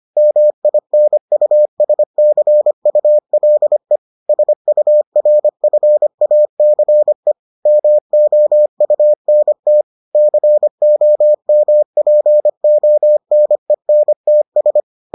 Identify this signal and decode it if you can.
Morse code, heard as a keyed tone, about 25 wpm, 600 Hz. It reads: MINUSCULE SURFACE MOUNT COMPONENTS